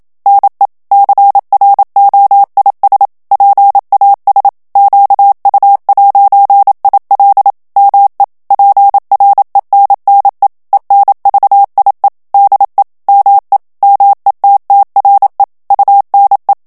Morse code, heard as a keyed tone, about 27 wpm, 800 Hz. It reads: NE CROIS PAS QU'IL ME PRENNE ENVIE DE ME METTRE UNE
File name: FRANCAIS TEXTE CLAIR 2 VITESSE 26 MOTS MINUTES